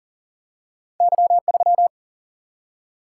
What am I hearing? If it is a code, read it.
Morse code, heard as a keyed tone, about 38 wpm, 700 Hz. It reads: Y3
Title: Single-Letter Single-Number 40wpm